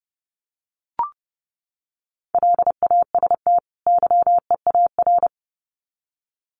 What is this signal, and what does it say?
Morse code, heard as a keyed tone, about 30 wpm, 700 Hz. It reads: LAST YEAR
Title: Sets of 2 Words 30wpm